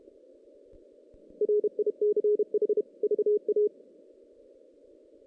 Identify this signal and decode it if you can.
Morse code, heard as a keyed tone, about 36 wpm, 410 Hz. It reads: RICH VA
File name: 2016 JAN CW NAQP / 2016 Jan naqp k5ot - 0-5